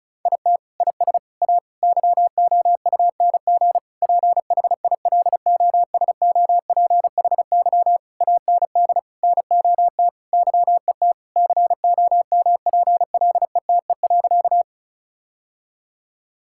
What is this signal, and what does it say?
Morse code, heard as a keyed tone, about 34 wpm, 700 Hz. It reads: IT IS A YOUNG PHILOSOPHY AND NOT YET COMPLETE.
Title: Sentences from Top 1000 Words - Review Part 6 - 35wpm